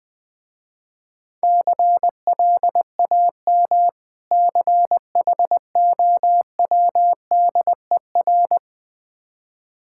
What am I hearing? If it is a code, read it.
Morse code, heard as a keyed tone, about 21 wpm, 690 Hz. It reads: CLAM CHOWDER
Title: Difficult Words with Hints - 20wpm